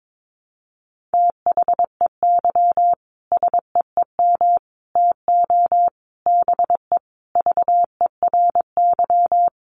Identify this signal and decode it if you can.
Morse code, heard as a keyed tone, about 22 wpm, 700 Hz. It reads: THEY SEEM TO BE VERY
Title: Sentences from Top 1100 Words 22wpm